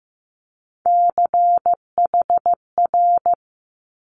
Morse code, heard as a keyed tone, about 15 wpm, 700 Hz. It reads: CHR
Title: Multi-letter Phonemes Quick Fire 15wpm